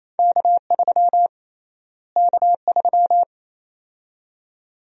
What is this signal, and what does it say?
Morse code, heard as a keyed tone, about 28 wpm, 700 Hz. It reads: K3 K3
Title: USA Call Sign Prefixes - Repeated 2x - 28wpm